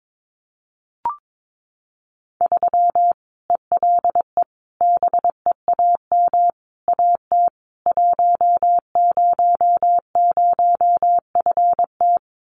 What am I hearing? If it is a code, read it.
Morse code, heard as a keyed tone, about 22 wpm, 700 Hz. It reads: 3 ELE BEAM AT 100FT
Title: Common QSO Phrases 22wpm